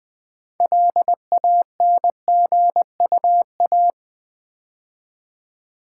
Morse code, heard as a keyed tone, about 20 wpm, 700 Hz. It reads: LANGUA
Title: Build A Word - 2 Letters At A Time - 8-Letter Words - 20wpm